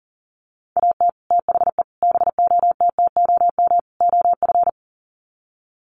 Morse code, heard as a keyed tone, about 40 wpm, 700 Hz. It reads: AT THE BOTTOM OF
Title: Sets of 4 Words 40wpm